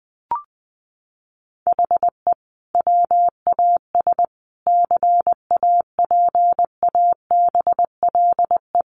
Morse code, heard as a keyed tone, about 20 wpm, 700 Hz. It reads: HE WAS CAPABLE
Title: Sentences from Top 1200 Words 20wpm